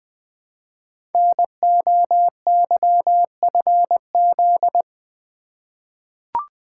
Morse code, heard as a keyed tone, about 21 wpm, 700 Hz. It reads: NOYFZ
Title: Code Groups - 5 Letters - 20wpm